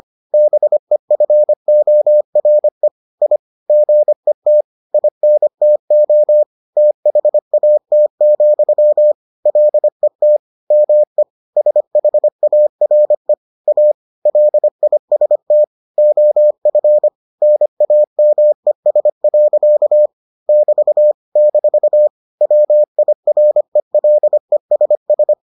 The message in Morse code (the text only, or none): BEFORE I GET INTO THAT, LET ME SHARE A LIST OF NAMES. <BT> - WIRELESS